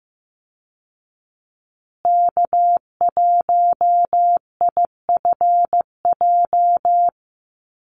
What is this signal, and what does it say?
Morse code, heard as a keyed tone, about 15 wpm, 700 Hz. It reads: K1IFJ
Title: Call Signs Repeated 3x - 15wpm